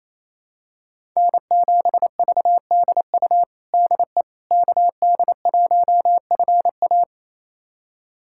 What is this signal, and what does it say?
Morse code, heard as a keyed tone, about 28 wpm, 700 Hz. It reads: N7VDU DE KD1FA
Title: USA Callsign DE Callsign - 5 Characters - 28wpm